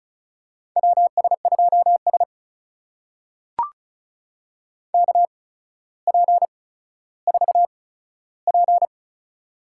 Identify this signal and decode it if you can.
Morse code, heard as a keyed tone, about 36 wpm, spaced as for 15 wpm, 700 Hz. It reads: WS2S KP4P
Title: USA Call Signs - 4 Characters - 8x Character Spacing - 35wpm